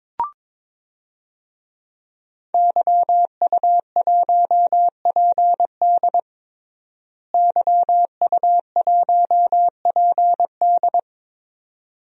Morse code, heard as a keyed tone, about 22 wpm, 700 Hz. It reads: YU1PD YU1PD
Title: DXCC-100 Call Signs - 5 Characters - Repeated 2x - 22wpm